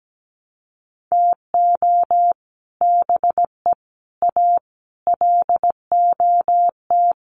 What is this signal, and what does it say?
Morse code, heard as a keyed tone, about 17 wpm, 700 Hz. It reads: TO BE A LOT
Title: Sets of 5 Words Encore 17wpm